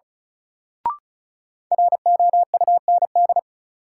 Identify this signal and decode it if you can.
Morse code, heard as a keyed tone, about 36 wpm, 700 Hz. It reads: ROUND
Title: Sets of 5 Words Encore - New Word Drill - 35wpm